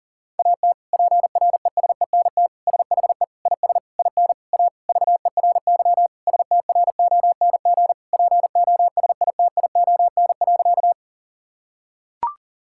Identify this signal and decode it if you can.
Morse code, heard as a keyed tone, about 37 wpm, 700 Hz. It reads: AT PRESENT SHE IS IN A VERY STRONG POSITION.
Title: Sentences from Top 1000 Words - Review Part 5 - 40wpm